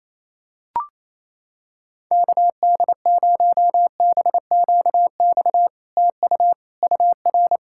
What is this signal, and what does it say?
Morse code, heard as a keyed tone, about 29 wpm, 700 Hz. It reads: KD0BQX TU UR
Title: SOTA QSO Practice 28wpm